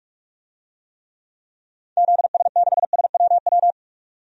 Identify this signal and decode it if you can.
Morse code, heard as a keyed tone, about 42 wpm, 700 Hz. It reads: ZS6SWW
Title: DXCC-100 Call Signs - 6 Characters - 45wpm